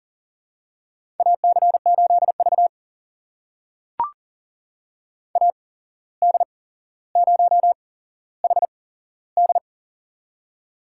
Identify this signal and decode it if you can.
Morse code, heard as a keyed tone, about 39 wpm, spaced as for 16 wpm, 700 Hz. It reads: AC8V AD0HD
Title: USA Call Signs - 8x Character Spacing - 40wpm